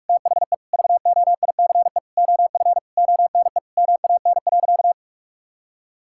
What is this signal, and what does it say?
Morse code, heard as a keyed tone, about 46 wpm, 700 Hz. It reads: THE VOICE OF ONE MAN.
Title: Sentences from Top 1000 Words - Review Part 6 - 45wpm